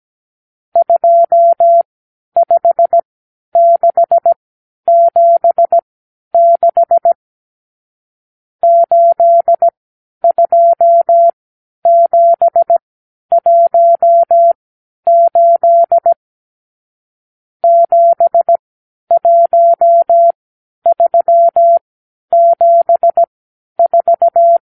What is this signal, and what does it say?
Morse code, heard as a keyed tone, about 17 wpm, spaced as for 10 wpm, 680 Hz. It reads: 25676 82718 71374
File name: k7qo / Lesson 54 (9)